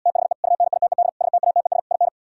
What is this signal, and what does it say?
Morse code, heard as a keyed tone, about 77 wpm, 690 Hz. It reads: THE UNITED STATES IN